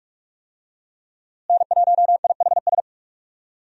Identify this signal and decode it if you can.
Morse code, heard as a keyed tone, about 46 wpm, 700 Hz. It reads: N1IHS
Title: USA Call Signs - 5 Characters - Repeated 3x - 45wpm